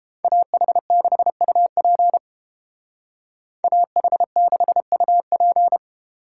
Morse code, heard as a keyed tone, about 29 wpm, 700 Hz. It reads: AH6UP AH6UP